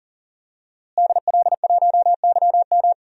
Morse code, heard as a keyed tone, about 40 wpm, 700 Hz. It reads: DL1YM
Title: DXCC-100 Call Signs - 5 Characters - Repeated 2x - 40wpm